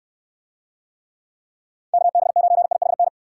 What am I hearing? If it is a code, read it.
Morse code, heard as a keyed tone, about 69 wpm, 700 Hz. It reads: KB9IHN